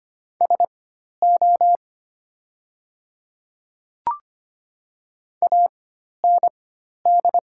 Morse code, heard as a keyed tone, about 25 wpm, spaced as for 12 wpm, 700 Hz. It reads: SO AND
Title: 4x Character Spacing - Top 100 Words - 25wpm